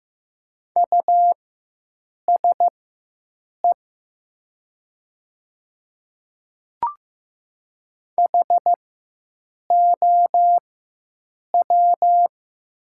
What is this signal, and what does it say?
Morse code, heard as a keyed tone, about 15 wpm, spaced as for 7 wpm, 690 Hz. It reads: USE HOW
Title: 4x Character Spacing - Top 500 Words - 15wpm